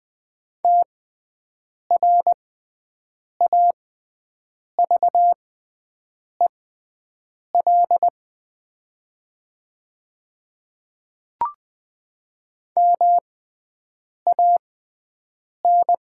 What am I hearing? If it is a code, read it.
Morse code, heard as a keyed tone, about 20 wpm, spaced as for 7 wpm, 700 Hz. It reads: TRAVEL MAN
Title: 6x Character Spacing - Top 500 Words - 20wpm